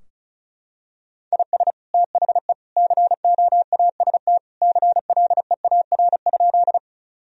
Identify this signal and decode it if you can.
Morse code, heard as a keyed tone, about 34 wpm, 700 Hz. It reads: IS THE COAST CLEAR?
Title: Sentences from Top 1000 Words - Review Part 4 - 35wpm